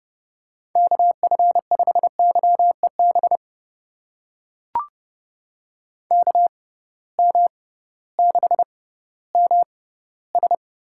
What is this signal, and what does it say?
Morse code, heard as a keyed tone, about 30 wpm, spaced as for 16 wpm, 700 Hz. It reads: KF5YEB KM6MS
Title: USA Call Signs - 6 Characters - 6x Character Spacing - 30wpm